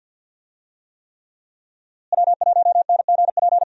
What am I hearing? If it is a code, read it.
Morse code, heard as a keyed tone, about 52 wpm, 700 Hz. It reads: W1NGP
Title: USA Call Signs - 5 Characters - Repeated 3x - 50wpm